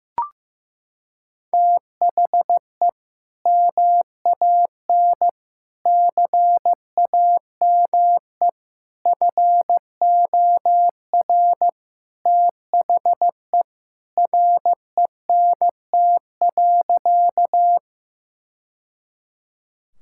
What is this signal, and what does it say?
Morse code, heard as a keyed tone, about 15 wpm, 700 Hz. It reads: THE MAN CAME FOR THE RENT.
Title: Sentences from Top 1000 Words - Review Part 5 - 15wpm